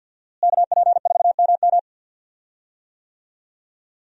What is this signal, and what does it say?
Morse code, heard as a keyed tone, about 48 wpm, 700 Hz. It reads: KP4MM
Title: USA Call Signs - 5 Characters - Repeated 2x - 50wpm